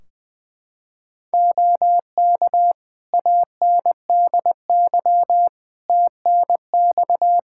Morse code, heard as a keyed tone, about 20 wpm, 700 Hz. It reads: OK ANDY TNX
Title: K1USN SST QSO Practice 20wpm